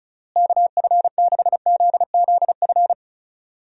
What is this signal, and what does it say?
Morse code, heard as a keyed tone, about 36 wpm, 700 Hz. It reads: KF6ZZF
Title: USA Call Signs - 6 Characters - Repeated 3x - 35wpm